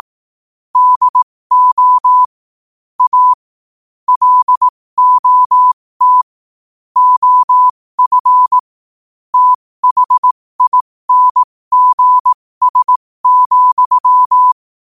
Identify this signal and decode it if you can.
Morse code, heard as a keyed tone, about 18 wpm, spaced as for 15 wpm, 1000 Hz. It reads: DO A LOT OF THINGS,
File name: qotdfiles / 15wpm